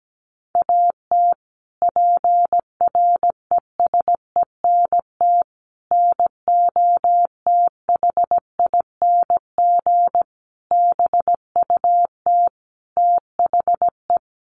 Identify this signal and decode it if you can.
Morse code, heard as a keyed tone, about 17 wpm, 700 Hz. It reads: AT PRESENT NOTHING BUT THE
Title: Sentences from Top 1500 Words 17wpm